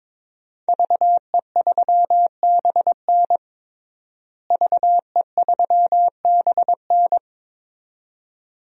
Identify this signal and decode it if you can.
Morse code, heard as a keyed tone, about 23 wpm, 700 Hz. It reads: VE3BN VE3BN
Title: DXCC-100 Call Signs - 5 Characters - Repeated 2x - 22wpm